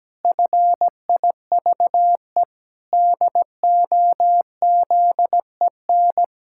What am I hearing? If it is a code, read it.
Morse code, heard as a keyed tone, about 17 wpm, 700 Hz. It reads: FIVE DOZEN
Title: Pangram Challenge v2 - 17wpm